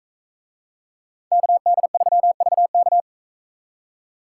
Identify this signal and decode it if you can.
Morse code, heard as a keyed tone, about 40 wpm, 700 Hz. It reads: KD3VK